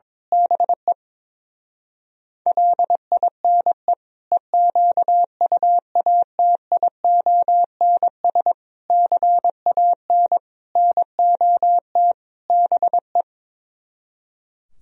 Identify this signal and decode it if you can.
Morse code, heard as a keyed tone, about 22 wpm, 700 Hz. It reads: BE LINE EQUATIONS CAN NOT BE
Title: Speed-Racing - Sets of 5 Words 22wpm